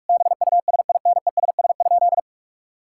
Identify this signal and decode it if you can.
Morse code, heard as a keyed tone, about 43 wpm, 700 Hz. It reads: BUSINESS?
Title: Sentences from Top 1600 Words 45wpm